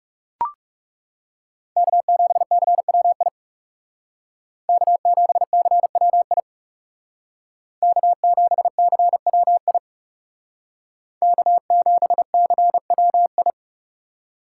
Morse code, heard as a keyed tone, about 40 wpm, 700 Hz. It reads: K7CWI K7CWI K7CWI K7CWI